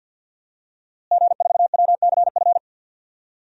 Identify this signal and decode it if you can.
Morse code, heard as a keyed tone, about 50 wpm, 700 Hz. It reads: G4WCF
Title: DXCC-100 Call Signs - 5 Characters - Repeated 3x - 50wpm